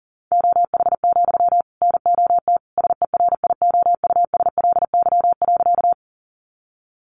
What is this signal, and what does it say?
Morse code, heard as a keyed tone, about 40 wpm, 700 Hz. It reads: OH, NOT SERIOUSLY.
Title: Sentences from Top 1200 Words 40wpm